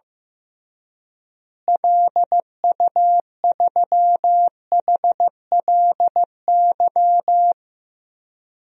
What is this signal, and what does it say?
Morse code, heard as a keyed tone, about 15 wpm, 700 Hz. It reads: LU3HLY
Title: DXCC-100 Call Signs - 6 Characters - Repeated 3x - 15wpm